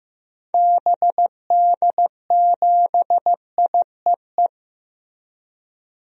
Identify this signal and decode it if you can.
Morse code, heard as a keyed tone, about 15 wpm, 700 Hz. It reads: BD7IEE
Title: DXCC-100 Call Signs - 6 Characters - Repeated 2x - 15wpm